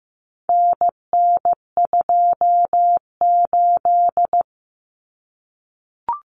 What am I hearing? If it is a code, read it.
Morse code, heard as a keyed tone, about 15 wpm, 700 Hz. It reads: NN28
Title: Common Cut Numbers - 4-Digit 15wpm